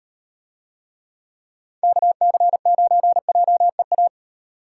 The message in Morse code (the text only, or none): KC9JEA